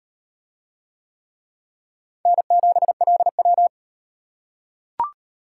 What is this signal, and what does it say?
Morse code, heard as a keyed tone, about 38 wpm, 700 Hz. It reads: N7LW